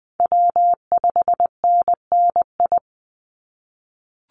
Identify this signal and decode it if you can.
Morse code, heard as a keyed tone, about 19 wpm, 700 Hz. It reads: W5NNI
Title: USA Call Signs 20wpm